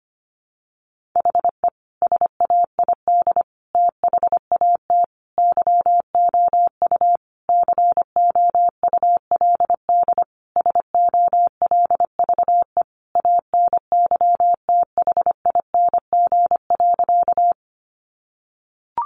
Morse code, heard as a keyed tone, about 25 wpm, 700 Hz. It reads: HE SAID THAT YOU COULD SOLVE ANYTHING.
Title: Sentences from Top 1000 Words 25wpm